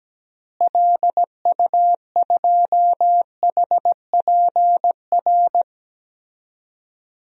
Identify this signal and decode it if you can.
Morse code, heard as a keyed tone, about 17 wpm, 700 Hz. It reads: LU2HPR
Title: DXCC-100 Call Signs - 6 Characters - Repeated 2x - 17wpm